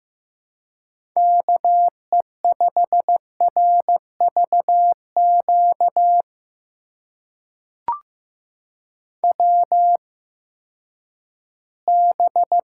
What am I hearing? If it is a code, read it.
Morse code, heard as a keyed tone, about 15 wpm, spaced as for 10 wpm, 700 Hz. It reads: KE5RVQ WB
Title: USA Call Signs - 6 Characters - 8x Character Spacing - 15wpm